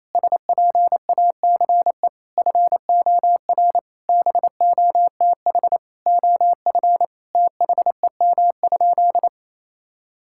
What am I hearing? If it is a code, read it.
Morse code, heard as a keyed tone, about 28 wpm, 700 Hz. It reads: SPACE FOR BOTH OF THEM?
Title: Sentences from Top 1000 Words - Review Part 5 - 28wpm